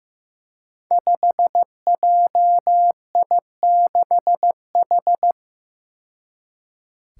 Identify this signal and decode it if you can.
Morse code, heard as a keyed tone, about 15 wpm, 700 Hz. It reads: 5JI6H
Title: Five Challenging Letter-Number Combinations 15wpm